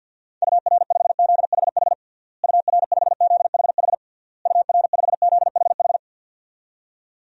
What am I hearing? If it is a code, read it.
Morse code, heard as a keyed tone, about 50 wpm, 700 Hz. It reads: UR5ZHH UR5ZHH UR5ZHH
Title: DXCC-100 Call Signs - 6 Characters - Repeated 3x - 50wpm